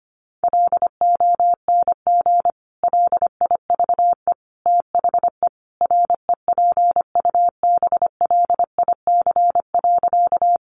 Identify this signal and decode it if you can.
Morse code, heard as a keyed tone, about 25 wpm, 700 Hz. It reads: LONG LIVE THE REPUBLIC.
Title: Sentences from Top 1500 Words 25wpm